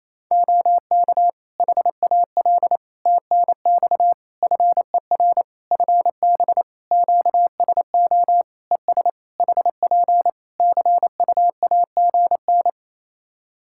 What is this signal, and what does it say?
Morse code, heard as a keyed tone, about 28 wpm, 700 Hz. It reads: OK HAL TNX FER FB QSO ES HP CUAGN
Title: Practice QSO - Exchange 4 - Ending - 28wpm